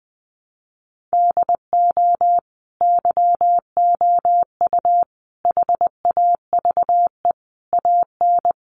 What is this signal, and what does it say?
Morse code, heard as a keyed tone, about 20 wpm, 700 Hz. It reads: DO YOU HAVE AN
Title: Sentences from Top 1300 Words 20wpm